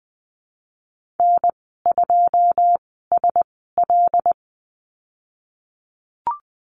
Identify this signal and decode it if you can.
Morse code, heard as a keyed tone, about 20 wpm, spaced as for 15 wpm, 700 Hz. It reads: N2SL
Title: 128 Lesson 21g - Number 6 - Call Signs 20wpm